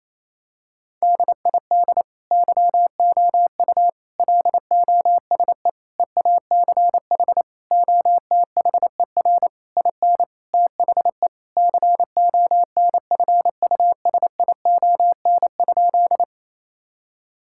Morse code, heard as a keyed tone, about 28 wpm, 700 Hz. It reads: DID YOU LOSE EACH OTHER IN THE CONFUSION?
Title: Sentences from Top 1600 Words 28wpm